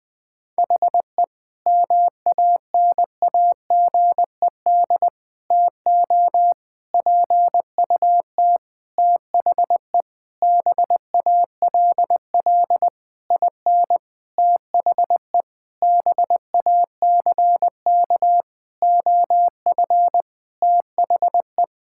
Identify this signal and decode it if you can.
Morse code, heard as a keyed tone, about 20 wpm, 700 Hz. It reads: HE MANAGED TO PUT THE BALL IN THE BACK OF THE
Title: Sentences with 9-Word Phrases - Spoken Beginning - 20wpm